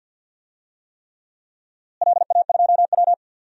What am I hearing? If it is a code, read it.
Morse code, heard as a keyed tone, about 46 wpm, 700 Hz. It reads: LA2W
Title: DXCC-100 Call Signs - 4 Characters - Repeated 3x - 50wpm